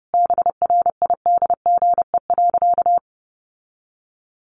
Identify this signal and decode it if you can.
Morse code, heard as a keyed tone, about 30 wpm, 700 Hz. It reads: BRIDGE.
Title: Sentences from Top 1400 Words 30wpm